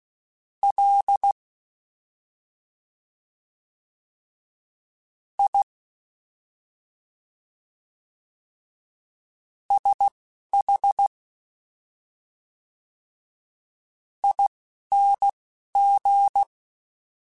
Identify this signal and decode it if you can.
Morse code, heard as a keyed tone, about 16 wpm, spaced as for 4 wpm, 800 Hz. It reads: LISHING
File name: Special INGs-mp3s / Special INGs-mp3s / Slow ing @10